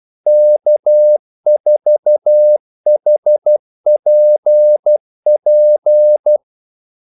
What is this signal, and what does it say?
Morse code, heard as a keyed tone, about 12 wpm, 600 Hz. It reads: K4HPP
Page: 2 ekstra ordmellemrum mellem kaldesignalerne